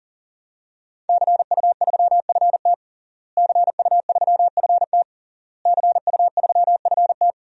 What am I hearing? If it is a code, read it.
Morse code, heard as a keyed tone, about 40 wpm, 700 Hz. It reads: CU3FT CU3FT CU3FT
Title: DXCC-100 Call Signs - 5 Characters - Repeated 3x - 40wpm